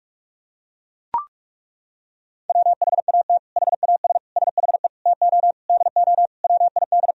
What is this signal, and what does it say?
Morse code, heard as a keyed tone, about 41 wpm, 700 Hz. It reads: WHAT HAS SHE TO DO WID
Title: Sentences from Top 100 Words and Abbreviations 45wpm